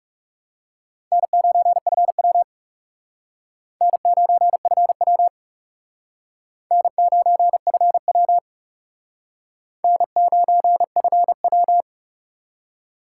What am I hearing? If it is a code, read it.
Morse code, heard as a keyed tone, about 37 wpm, 700 Hz. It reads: N9FW N9FW N9FW N9FW
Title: Speed Racing - USA Call Signs 30wpm